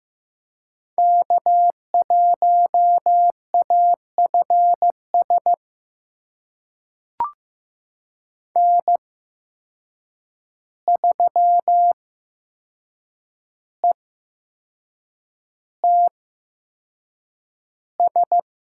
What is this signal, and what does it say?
Morse code, heard as a keyed tone, about 15 wpm, spaced as for 6 wpm, 700 Hz. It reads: K1AFS N3ETS
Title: USA Call Signs - 5 Characters - 8x Character Spacing - 15wpm